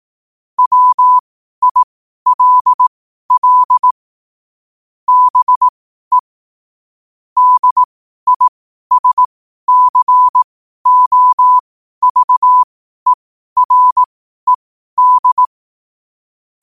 Morse code, heard as a keyed tone, about 18 wpm, spaced as for 12 wpm, 1000 Hz. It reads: WILL BE DISCOVERED
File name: qotdfiles / 12wpm